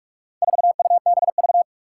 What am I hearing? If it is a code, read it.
Morse code, heard as a keyed tone, about 46 wpm, 700 Hz. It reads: 4UBV
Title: Four Challenging Letter-Number Combinations 45wpm